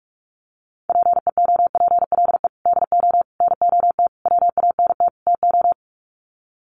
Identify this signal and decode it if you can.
Morse code, heard as a keyed tone, about 44 wpm, 700 Hz. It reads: PEOPLE DO NOT WANT TO
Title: Sets of 5 Words Encore2 45wpm